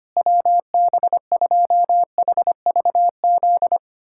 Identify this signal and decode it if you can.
Morse code, heard as a keyed tone, about 25 wpm, 700 Hz. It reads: WB2HVZ
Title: USA Callsign DE Callsign 25wpm